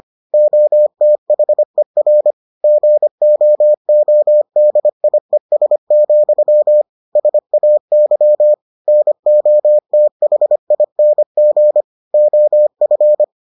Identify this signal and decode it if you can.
Morse code, heard as a keyed tone, about 25 wpm, 600 Hz. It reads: OTHER GOODIES, SAY NOTHING OF